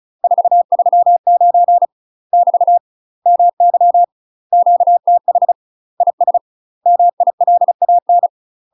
Morse code, heard as a keyed tone, about 34 wpm, 700 Hz. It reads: 439 <BT> MY QTH IS MILAN
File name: UK MASTER QSOs TRAINING / 35WPM / UK MASTER QSOs 0043